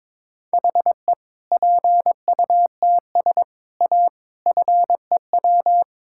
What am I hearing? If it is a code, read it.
Morse code, heard as a keyed tone, about 22 wpm, 700 Hz. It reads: HE PUTS A FEW
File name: Difficult Sentences with Hints - 22wpm